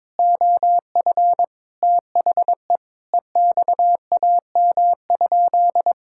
Morse code, heard as a keyed tone, about 22 wpm, 700 Hz. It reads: OF THE EXAM?
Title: Sentences from Top 1000 Words - Review Part 5 - 22wpm